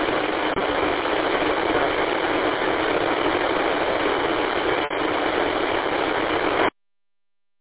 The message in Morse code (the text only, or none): none